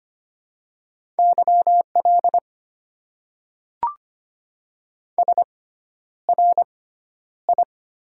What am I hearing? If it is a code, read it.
Morse code, heard as a keyed tone, about 25 wpm, spaced as for 11 wpm, 690 Hz. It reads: YL SRI
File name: Top 90 QSO Elements - 6x Character Spacing - 25wpm